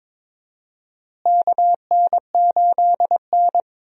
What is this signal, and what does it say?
Morse code, heard as a keyed tone, about 22 wpm, 700 Hz. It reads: KN8N
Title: USA Call Signs - 4 Characters - Repeated 2x - 22wpm